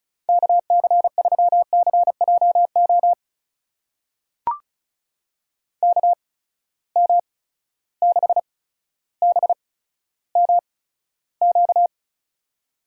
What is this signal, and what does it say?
Morse code, heard as a keyed tone, about 36 wpm, spaced as for 15 wpm, 700 Hz. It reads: KC3CJO KM6BMQ
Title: USA Call Signs - 6 Characters - 8x Character Spacing - 35wpm